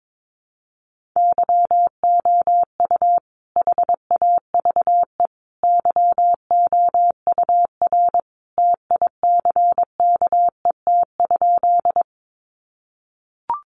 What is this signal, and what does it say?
Morse code, heard as a keyed tone, about 22 wpm, 700 Hz. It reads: YOU HAVE YOUR TICKET?
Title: Sentences from Top 1400 Words 22wpm